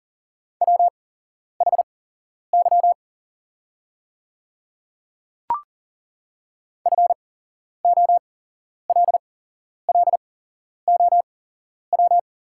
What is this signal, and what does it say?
Morse code, heard as a keyed tone, about 40 wpm, spaced as for 11 wpm, 690 Hz. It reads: WHY FOLLOW
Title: 8x Character Spacing - Top 200 Words - 40wpm